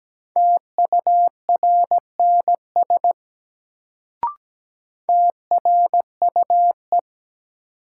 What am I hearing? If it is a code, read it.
Morse code, heard as a keyed tone, about 17 wpm, 700 Hz. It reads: TURNS TRUE
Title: Sets of 5 Words Encore - New Word Drill - 17wpm